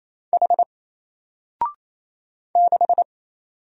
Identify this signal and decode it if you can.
Morse code, heard as a keyed tone, about 27 wpm, 700 Hz. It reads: H 6
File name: Single-Letter-Number-28wpm